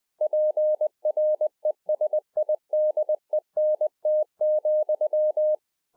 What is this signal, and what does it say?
Morse code, heard as a keyed tone, about 20 wpm, 610 Hz. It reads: PRESIDENT,